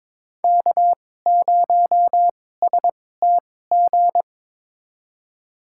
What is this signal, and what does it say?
Morse code, heard as a keyed tone, about 21 wpm, spaced as for 16 wpm, 700 Hz. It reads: K0STG
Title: USA Call Signs - 5 Characters - 2x Character Spacing - 22wpm